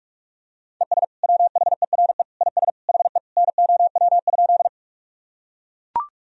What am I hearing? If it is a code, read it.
Morse code, heard as a keyed tone, about 41 wpm, 700 Hz. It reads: ES WHERE IS HE NOW?
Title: Sentences from Top 200 Words and Abbreviations 45wpm